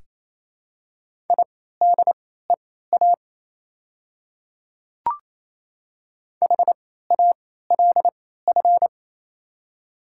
Morse code, heard as a keyed tone, about 27 wpm, spaced as for 16 wpm, 700 Hz. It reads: IDEA HALF
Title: 3x Character Spacing - Top 500 Words - 28wpm